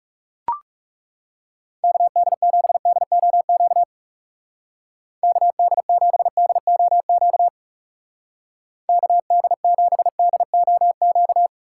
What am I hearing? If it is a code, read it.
Morse code, heard as a keyed tone, about 36 wpm, 700 Hz. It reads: KD7DOQ KD7DOQ KD7DOQ
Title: Speed Racing - USA 2x3 Call Signs 30wpm